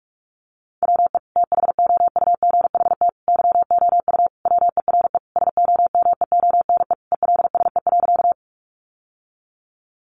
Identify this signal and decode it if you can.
Morse code, heard as a keyed tone, about 44 wpm, 700 Hz. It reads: WE THOUGHT YOU WERE SOMEONE ELSE.
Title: Sentences from Top 800 Words 45wpm